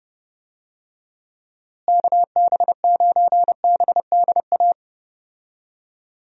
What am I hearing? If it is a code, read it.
Morse code, heard as a keyed tone, about 31 wpm, 700 Hz. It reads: KB9BDA
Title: USA Call Signs - 6 Characters - Repeated 2x - 30wpm